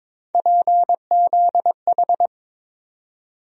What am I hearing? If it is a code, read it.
Morse code, heard as a keyed tone, about 23 wpm, 700 Hz. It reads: PZH
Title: USA Call Sign Postfixes - 3 Letters 22wpm